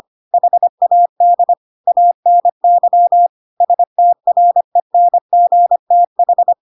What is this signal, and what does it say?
Morse code, heard as a keyed tone, about 25 wpm, 700 Hz. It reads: HAD ANY STRENGTH